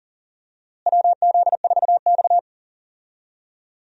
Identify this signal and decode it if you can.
Morse code, heard as a keyed tone, about 38 wpm, 700 Hz. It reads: WZ4X
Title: USA Call Signs - 4 Characters - Repeated 3x - 40wpm